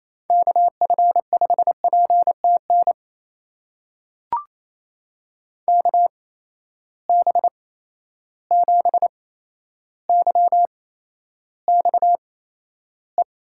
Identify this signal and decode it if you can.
Morse code, heard as a keyed tone, about 28 wpm, spaced as for 12 wpm, 700 Hz. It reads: KF5PTN KB7YXE
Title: USA Call Signs - 6 Characters - 8x Character Spacing - 28wpm